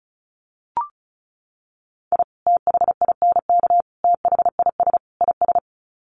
Morse code, heard as a keyed tone, about 34 wpm, 700 Hz. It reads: I THINK THIS IS
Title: Sets of 4 Words 35wpm